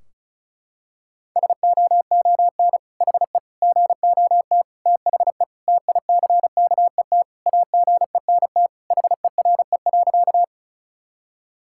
Morse code, heard as a keyed tone, about 36 wpm, 700 Hz. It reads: SOON HE GOT THE TICKET AGENT HERE.
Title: Sentences from Top 1600 Words 35wpm